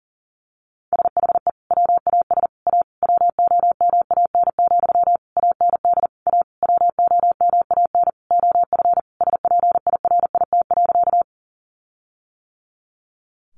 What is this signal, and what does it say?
Morse code, heard as a keyed tone, about 40 wpm, 700 Hz. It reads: SHE WAS A WOMAN, AND A WOMAN OF SPIRIT.
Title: Sentences from Top 1400 Words 40wpm